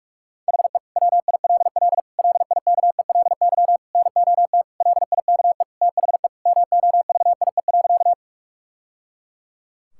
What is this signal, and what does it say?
Morse code, heard as a keyed tone, about 41 wpm, 700 Hz. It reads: HE WILL LIKELY NOT LIKE THE MOVIE.
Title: Sentences from Top 1000 Words - Review Part 5 - 45wpm